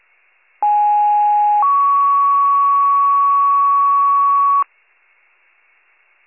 Two-tone page, heard as a one-second tone, then a three-second tone, about 830 then 1120 Hz. It is heard from 0.6 to 4.6 s.